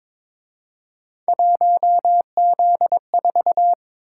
Morse code, heard as a keyed tone, about 26 wpm, 700 Hz. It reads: 1Z4
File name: Speed-Racing - Three Letter-Number Combinations - 22wpm